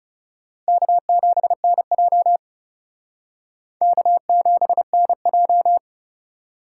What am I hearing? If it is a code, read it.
Morse code, heard as a keyed tone, about 37 wpm, 700 Hz. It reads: K7NJ K7NJ
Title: Speed Racing - USA 1x2 Call Signs 30wpm